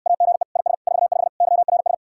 Morse code, heard as a keyed tone, about 69 wpm, 700 Hz. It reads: AGE IS 45 YRS